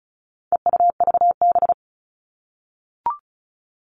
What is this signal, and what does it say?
Morse code, heard as a keyed tone, about 34 wpm, 700 Hz. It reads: EUVB
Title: All Cut Numbers - 4-Digit 35wpm